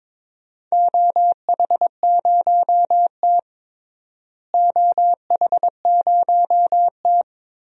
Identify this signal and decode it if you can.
Morse code, heard as a keyed tone, about 22 wpm, 700 Hz. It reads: OH0T OH0T
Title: DXCC-100 Call Signs - 4 Characters - Repeated 2x - 22wpm